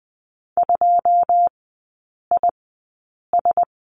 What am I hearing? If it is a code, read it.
Morse code, heard as a keyed tone, about 20 wpm, spaced as for 8 wpm, 700 Hz. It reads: 2IS
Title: 093 Lesson 16e - Letter c - Three Characters 20wpm